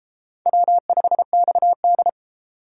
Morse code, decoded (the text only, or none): W5XD